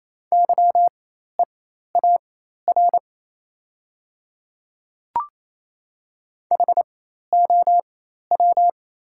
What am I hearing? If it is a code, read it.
Morse code, heard as a keyed tone, about 28 wpm, spaced as for 13 wpm, 700 Hz. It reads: YEAR HOW
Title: 4x Character Spacing - Top 200 Words - 28wpm